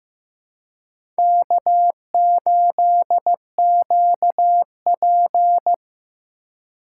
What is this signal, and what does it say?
Morse code, heard as a keyed tone, about 18 wpm, 700 Hz. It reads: K8QP
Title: Speed Racing - USA 1x2 Call Signs 15wpm